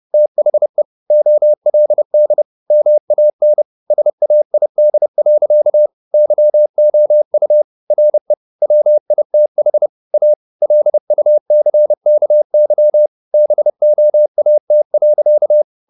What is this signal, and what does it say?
Morse code, heard as a keyed tone, about 30 wpm, 600 Hz. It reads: THE OLD MAN SAID. YOU RE WITH A LUCKY BOAT.
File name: the old man and the sea / the old man and the sea 1 30wpm